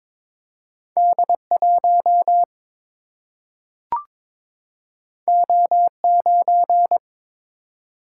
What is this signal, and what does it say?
Morse code, heard as a keyed tone, about 23 wpm, 700 Hz. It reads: D 1 O 9
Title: Single-Letter Single-Number 22wpm